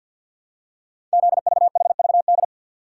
Morse code, heard as a keyed tone, about 48 wpm, 690 Hz. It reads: ZVHVD
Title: Code Groups - 5 Letters - 50wpm